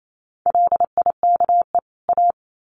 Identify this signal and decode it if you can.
Morse code, heard as a keyed tone, about 28 wpm, 700 Hz. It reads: LIKE A
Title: Sets of 2 Words 28wpm